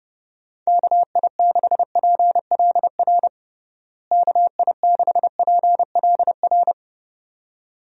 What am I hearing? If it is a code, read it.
Morse code, heard as a keyed tone, about 30 wpm, 700 Hz. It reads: KI6PLR KI6PLR
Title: USA Call Signs - 6 Characters - Repeated 2x - 30wpm